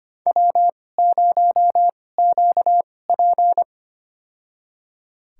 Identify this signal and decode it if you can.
Morse code, heard as a keyed tone, about 25 wpm, spaced as for 18 wpm, 700 Hz. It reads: W0QP
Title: USA Call Signs - 4 Characters - 2x Character Spacing - 25wpm